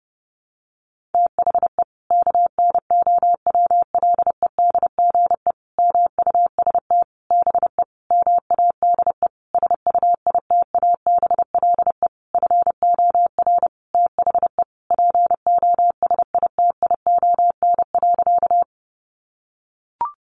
Morse code, heard as a keyed tone, about 30 wpm, 700 Hz. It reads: THE KNOWLEDGE MUST BE MADE SUITABLE FOR THE POSITION.
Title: Sentences from Top 1400 Words 30wpm